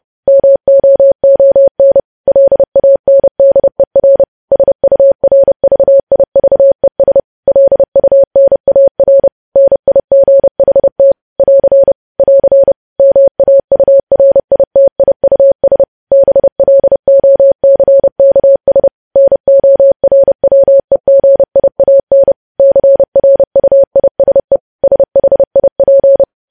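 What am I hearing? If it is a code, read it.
Morse code, heard as a keyed tone, about 30 wpm, 550 Hz. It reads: MOON LANDER SURVIVES LUNAR NIGHT <AR> <AR> MAURITIUS BLOCKS NORWEGIAN CRUISE SHIP
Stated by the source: News headlines sent in Morse at 30 words per minute.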